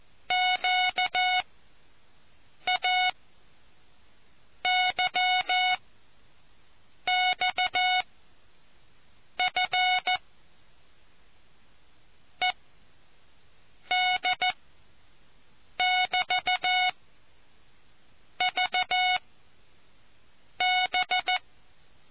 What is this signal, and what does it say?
Morse code, heard as a keyed tone, about 14 wpm, spaced as for 5 wpm, 730 Hz. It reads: QAYXF ED <BT> VB